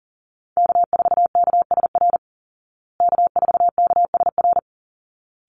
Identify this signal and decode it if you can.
Morse code, heard as a keyed tone, about 40 wpm, 700 Hz. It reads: K4KSR K4KSR
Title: Call Signs Repeated 3x - 40wpm